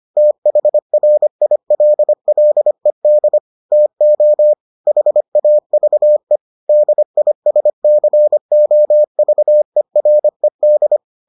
Morse code, heard as a keyed tone, about 25 wpm, 600 Hz. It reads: THRILLED TO HAVE DISCOVERED